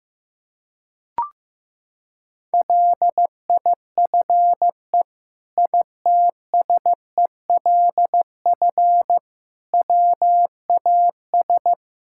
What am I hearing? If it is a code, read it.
Morse code, heard as a keyed tone, about 16 wpm, 700 Hz. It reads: LIFE ITSELF WAS
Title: Sentences from Top 1000 Words - Review Part 5 - 15wpm